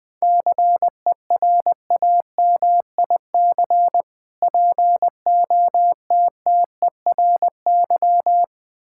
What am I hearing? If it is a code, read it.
Morse code, heard as a keyed tone, about 21 wpm, 700 Hz. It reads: CERAMIC POTTERY
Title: Difficult Words - Speed-Racing with Hints - 20wpm